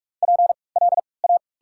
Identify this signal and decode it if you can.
Morse code, heard as a keyed tone, about 42 wpm, 700 Hz. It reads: P L A
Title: 3x Character Spacing - Top 200 Words - 45wpm